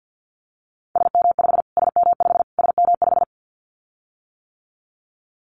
Morse code, heard as a keyed tone, about 48 wpm, 700 Hz. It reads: SM5 SM5 SM5
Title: Callsign Prefixes for Top 25 DXCC - Repeated 3x - 50wpm